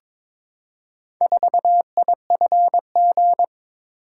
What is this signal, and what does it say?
Morse code, heard as a keyed tone, about 22 wpm, 700 Hz. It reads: 4IFG
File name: practice / 22wpm / Four Letter-Number Combinations - 22wpm